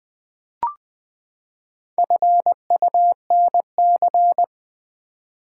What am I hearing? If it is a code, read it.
Morse code, heard as a keyed tone, about 20 wpm, 700 Hz. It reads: FUNC
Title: Build A Word - 4 Letters At A Time - 10-Letter Words - 20wpm